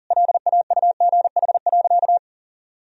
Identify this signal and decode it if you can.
Morse code, heard as a keyed tone, about 40 wpm, 700 Hz. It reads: LAUGH.
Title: Sentences from Top 1000 Words - Review Part 2 - 40wpm